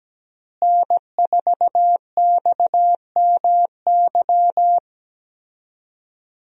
Morse code, heard as a keyed tone, about 17 wpm, 700 Hz. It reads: N4XMY
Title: USA Call Signs - 5 Characters - Repeated 3x - 17wpm